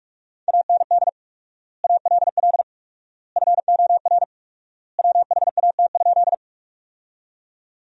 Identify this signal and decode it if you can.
Morse code, heard as a keyed tone, about 45 wpm, 700 Hz. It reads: AND ALL FOR WHAT?
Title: 4x Word-Spacing - Sentences from Top 100 Words - 45wpm